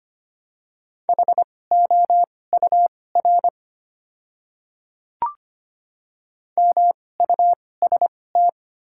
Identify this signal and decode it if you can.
Morse code, heard as a keyed tone, about 25 wpm, spaced as for 18 wpm, 700 Hz. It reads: HOUR MUST
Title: 2x Character Spacing - Top 500 Words - 25wpm